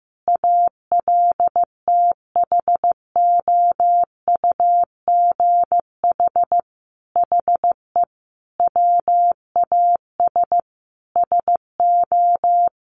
Morse code, heard as a keyed tone, about 15 wpm, 700 Hz. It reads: ALTHOUGH HE WAS SO
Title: Sentences from Top 600 Words 15wpm